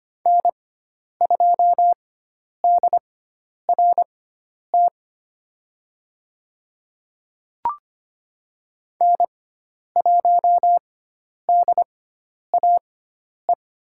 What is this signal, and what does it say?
Morse code, heard as a keyed tone, about 25 wpm, spaced as for 10 wpm, 700 Hz. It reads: N2DRT N1DAE
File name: Lesson 16g - Letter c - Call Signs 25wpm